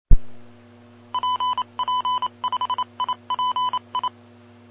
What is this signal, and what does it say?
Morse code, heard as a keyed tone, about 28 wpm, 1000 Hz. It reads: PP5IPI
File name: pp5bar / aulas de telegrafia / Aula-09